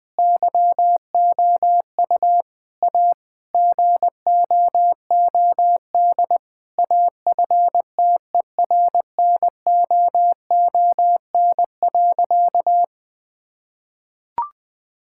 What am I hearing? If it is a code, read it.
Morse code, heard as a keyed tone, about 20 wpm, 700 Hz. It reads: YOU A GOOD AFTERNOON.
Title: Sentences from Top 1600 Words 20wpm